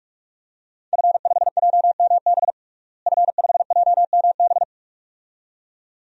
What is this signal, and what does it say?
Morse code, heard as a keyed tone, about 41 wpm, 700 Hz. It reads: F5JMB F5JMB
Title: DXCC-100 Call Signs - 5 Characters - Repeated 2x - 45wpm